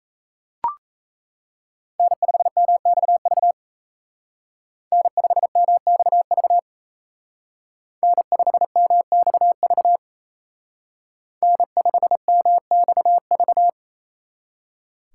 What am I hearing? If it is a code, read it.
Morse code, heard as a keyed tone, about 36 wpm, 700 Hz. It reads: N5MXV N5MXV N5MXV N5MXV